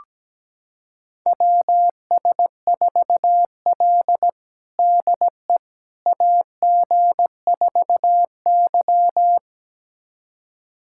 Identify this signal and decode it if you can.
Morse code, heard as a keyed tone, about 17 wpm, 700 Hz. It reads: WS4L DE AG4Y
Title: USA Callsign DE Callsign - 4 Characters - 17wpm